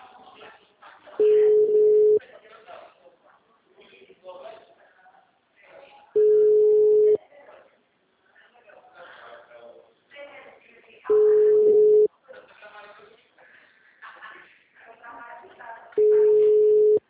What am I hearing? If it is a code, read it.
Ringback tone (Europe, 425 Hz), with speech.